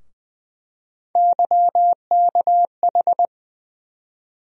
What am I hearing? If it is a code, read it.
Morse code, heard as a keyed tone, about 20 wpm, 700 Hz. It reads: YKH
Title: USA Call Sign Postfixes - 3 Letters 20wpm